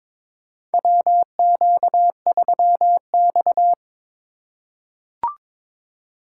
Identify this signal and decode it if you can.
Morse code, heard as a keyed tone, about 22 wpm, 700 Hz. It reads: WQ3X
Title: USA Call Signs - 4 Characters - 2x Character Spacing - 22wpm